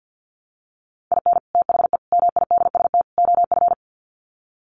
Morse code, heard as a keyed tone, about 50 wpm, 700 Hz. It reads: IN THE MIDST OF
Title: Sets of 4 Words 50wpm